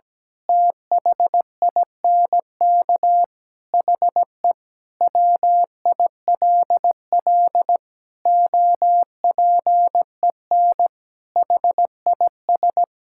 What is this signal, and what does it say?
Morse code, heard as a keyed tone, about 17 wpm, 700 Hz. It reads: THINK HE WILL OPEN HIS
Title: Sentences from Top 1600 Words 17wpm